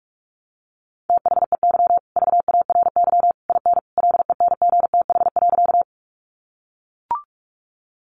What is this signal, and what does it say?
Morse code, heard as a keyed tone, about 44 wpm, 700 Hz. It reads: THEY VARY IN LENGTH.
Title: Sentences from Top 1200 Words 45wpm